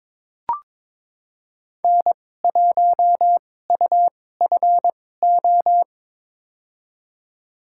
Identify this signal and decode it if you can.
Morse code, heard as a keyed tone, about 22 wpm, spaced as for 16 wpm, 700 Hz. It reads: N1UFO
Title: USA Call Signs - 5 Characters - 2x Character Spacing - 22wpm